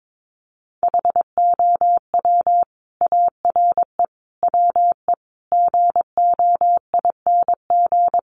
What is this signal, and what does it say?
Morse code, heard as a keyed tone, about 22 wpm, 700 Hz. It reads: HOW ARE WE GOING
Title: Sets of 5 Words Encore 22wpm